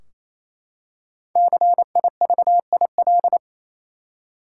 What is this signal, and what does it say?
Morse code, heard as a keyed tone, about 28 wpm, 700 Hz. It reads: CIVIL
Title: Five Letter Words - 28wpm